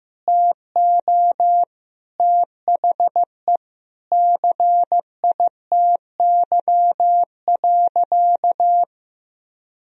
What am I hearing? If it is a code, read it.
Morse code, heard as a keyed tone, about 15 wpm, 700 Hz. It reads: TO THE CITY.
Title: Sentences from Top 1000 Words - Review Part 1 - 15wpm